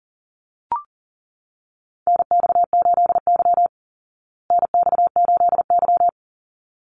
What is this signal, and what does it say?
Morse code, heard as a keyed tone, about 40 wpm, 700 Hz. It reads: NX8Y NX8Y
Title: Call Signs Repeated 3x - 40wpm